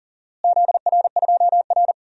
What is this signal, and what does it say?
Morse code, heard as a keyed tone, about 40 wpm, 700 Hz. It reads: ZR2R
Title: DXCC-100 Call Signs - 4 Characters - Repeated 3x - 40wpm